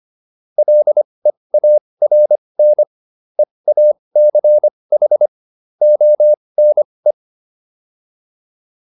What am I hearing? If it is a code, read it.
Morse code, heard as a keyed tone, about 25 wpm, spaced as for 20 wpm, 600 Hz. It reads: LEARN EACH ONE